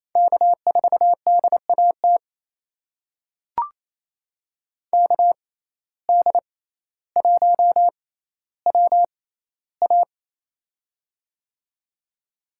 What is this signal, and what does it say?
Morse code, heard as a keyed tone, about 28 wpm, spaced as for 14 wpm, 700 Hz. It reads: K4DAT KD1WA
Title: USA Call Signs - 5 Characters - 6x Character Spacing - 28wpm